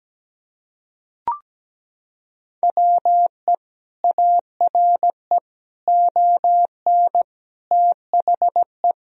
Morse code, heard as a keyed tone, about 17 wpm, 700 Hz. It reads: WE ARE ON THE
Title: Sentences from Top 1000 Words - Review Part 3 - 17wpm